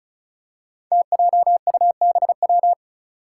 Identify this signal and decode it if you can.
Morse code, heard as a keyed tone, about 36 wpm, 700 Hz. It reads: TJUBW
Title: Five Challenging Letter-Number Combinations 35wpm